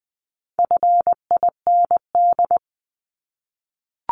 Morse code, heard as a keyed tone, about 20 wpm, 700 Hz. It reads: FIND
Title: ING Suffix - Top 100 Words with Base Words 20wpm